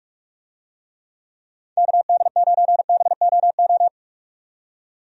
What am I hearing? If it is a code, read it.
Morse code, heard as a keyed tone, about 46 wpm, 700 Hz. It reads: KD9BOO
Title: USA Call Signs - 6 Characters - 45wpm